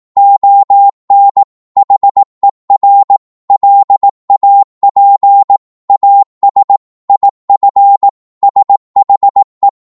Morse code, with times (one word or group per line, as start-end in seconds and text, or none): 0.2-1.4 ON
1.8-3.2 HER
3.5-5.6 LAP
5.9-6.8 AS
7.1-8.1 IF
8.4-9.7 SHE